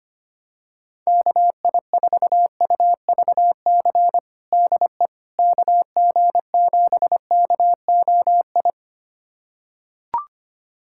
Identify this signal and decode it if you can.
Morse code, heard as a keyed tone, about 25 wpm, 700 Hz. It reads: KI4UVC DE KG7KOI
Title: USA Callsign DE Callsign - 6 Characters - 25wpm